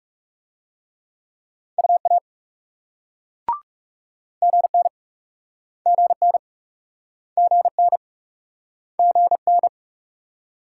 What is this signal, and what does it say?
Morse code, heard as a keyed tone, about 42 wpm, 700 Hz. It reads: UA GN GN GN GN